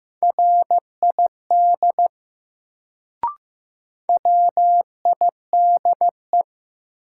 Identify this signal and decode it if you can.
Morse code, heard as a keyed tone, about 15 wpm, 700 Hz. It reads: RID WIDE
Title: Sets of 4 Words - New Word Drill - 15wpm